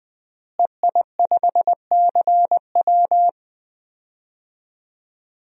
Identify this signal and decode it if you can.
Morse code, heard as a keyed tone, about 20 wpm, 700 Hz. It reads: EI5CW
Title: DXCC-100 Call Signs - 5 Characters - Repeated 2x - 20wpm